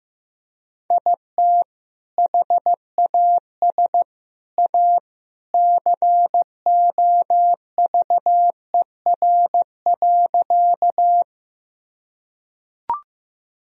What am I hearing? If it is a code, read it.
Morse code, heard as a keyed tone, about 15 wpm, 700 Hz. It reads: IT HAS A COVER.
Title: Sentences from Top 1000 Words - Review Part 1 - 15wpm